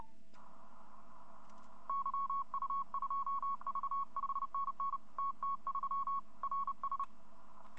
Morse code, heard as a keyed tone, about 30 wpm, 1100 Hz. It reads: YU2V 5NN TT3 RS
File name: hfkup2018 / greske / yu2v